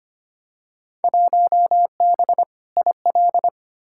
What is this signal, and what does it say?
Morse code, heard as a keyed tone, about 25 wpm, 700 Hz. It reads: 1B IL
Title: Field Day Exchanges - 25wpm